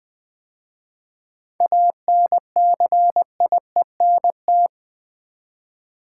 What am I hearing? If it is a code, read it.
Morse code, heard as a keyed tone, about 20 wpm, 700 Hz. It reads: ANCIENT
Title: 7 Letter Words - First 3 Letters Given - 20wpm